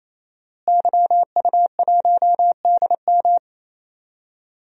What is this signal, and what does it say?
Morse code, heard as a keyed tone, about 28 wpm, 700 Hz. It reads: YU1DM
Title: DXCC-100 Call Signs - 5 Characters - Repeated 3x - 28wpm